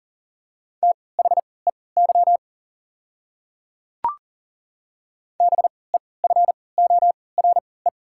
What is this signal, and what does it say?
Morse code, heard as a keyed tone, about 40 wpm, spaced as for 23 wpm, 690 Hz. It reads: THEY BEFORE
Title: 3x Character Spacing - Top 200 Words - 40wpm